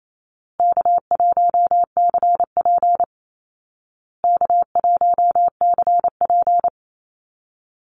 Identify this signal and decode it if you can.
Morse code, heard as a keyed tone, about 28 wpm, 700 Hz. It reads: K1CP K1CP
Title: Easy Call Signs Repeated 3x - 28wpm